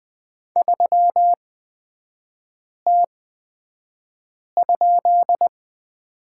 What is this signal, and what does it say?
Morse code, heard as a keyed tone, about 20 wpm, spaced as for 5 wpm, 690 Hz. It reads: 3T?
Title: 229 Lesson 37e - Letter x - Three Characters 20wpm